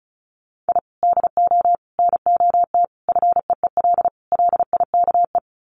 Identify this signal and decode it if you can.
Morse code, heard as a keyed tone, about 34 wpm, 700 Hz. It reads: I DO NOT FEEL LIKE
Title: Sets of 5 Words Encore 35wpm